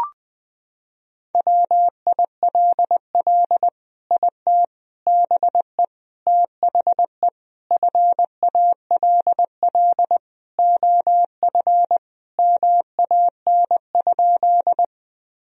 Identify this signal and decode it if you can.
Morse code, heard as a keyed tone, about 20 wpm, 700 Hz. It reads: WILL IT BE THE FALL OF MAN?
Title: Sentences from Top 1000 Words - Review Part 2 - 20wpm